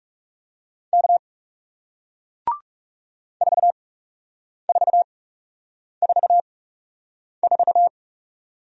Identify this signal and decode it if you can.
Morse code, heard as a keyed tone, about 38 wpm, spaced as for 8 wpm, 700 Hz. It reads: K 4444